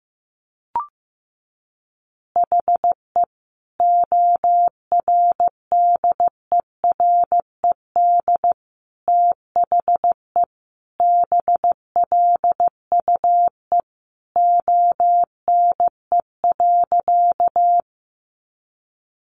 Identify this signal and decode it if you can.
Morse code, heard as a keyed tone, about 15 wpm, 700 Hz. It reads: HE ORDERED THE BLUE ONE.
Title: Additional Easy Phrases 15wpm